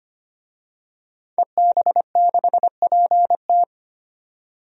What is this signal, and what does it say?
Morse code, heard as a keyed tone, about 25 wpm, 700 Hz. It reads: EB6PT
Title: DXCC-100 Call Signs - 5 Characters - 25wpm